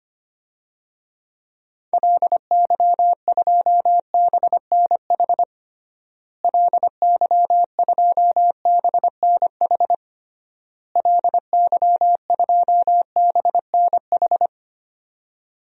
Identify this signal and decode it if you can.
Morse code, heard as a keyed tone, about 25 wpm, 700 Hz. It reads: LY2BNH LY2BNH LY2BNH
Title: DXCC-100 Call Signs - 6 Characters - Repeated 3x - 25wpm